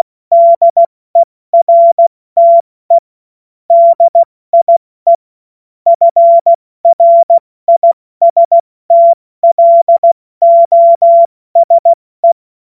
Morse code, heard as a keyed tone, about 16 wpm, spaced as for 14 wpm, 680 Hz. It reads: DERTE DIE FRISTLOSE
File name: Morsekurs / Witze / Witz 05 Dezember 2023 70 F80